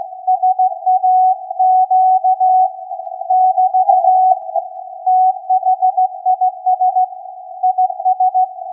Morse code, heard as a keyed tone, about 16 wpm, 740 Hz. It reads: SAQ <BT> THIS IS
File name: iz7slz / VLF / SAQ 16-11-2022